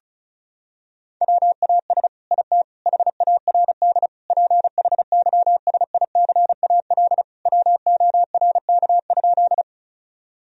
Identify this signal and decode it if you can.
Morse code, heard as a keyed tone, about 34 wpm, 700 Hz. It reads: WAS IT HARD PHYSICAL WORK?
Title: Sentences from Top 1000 Words - Review Part 4 - 35wpm